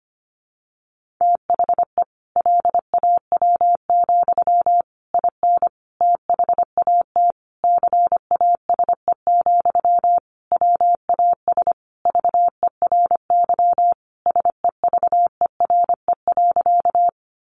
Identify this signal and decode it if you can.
Morse code, heard as a keyed tone, about 25 wpm, 700 Hz. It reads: THE LAW, IN THAT CASE, WAS VERY SEVERE.
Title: Sentences from Top 1400 Words 25wpm